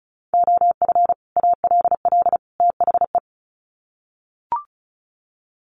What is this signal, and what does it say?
Morse code, heard as a keyed tone, about 34 wpm, 700 Hz. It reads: OF ALL THE
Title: Sets of 3 Words 35wpm